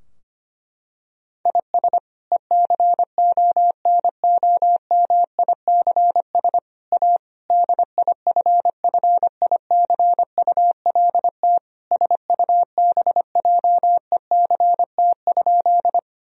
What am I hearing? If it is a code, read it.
Morse code, heard as a keyed tone, about 25 wpm, 700 Hz. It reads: IS ECONOMICS A DIFFICULT SUBJECT?
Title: Sentences from Top 1000 Words - Review Part 3 - 25wpm